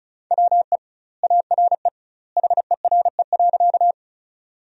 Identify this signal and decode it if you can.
Morse code, heard as a keyed tone, about 36 wpm, 700 Hz. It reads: WE ARE HERE.
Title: 2x Word-Spacing - Sentences from Top 200 Words - 35wpm